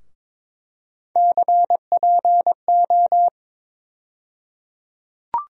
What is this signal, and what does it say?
Morse code, heard as a keyed tone, about 21 wpm, 700 Hz. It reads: CPO
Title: USA Call Sign Postfixes - 3 Letters 22wpm